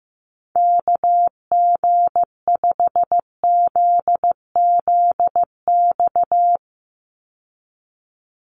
Morse code, heard as a keyed tone, about 15 wpm, 700 Hz. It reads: KG5ZZX
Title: USA Call Signs 15wpm